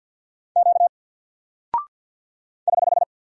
Morse code, heard as a keyed tone, about 48 wpm, 700 Hz. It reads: Q <HH>
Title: Single Character (Letter:Number:Punctuation:Prosign) Alt 50wpm